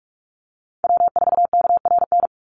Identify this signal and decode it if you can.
Morse code, heard as a keyed tone, about 44 wpm, 700 Hz. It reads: W4KRN
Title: NA SOTA Chaser Callsigns 45wpm